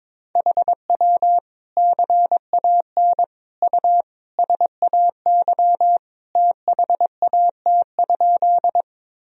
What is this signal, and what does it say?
Morse code, heard as a keyed tone, about 21 wpm, 700 Hz. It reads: HW CAN U SAY THAT?
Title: Sentences from Top 200 Words and Abbreviations 22wpm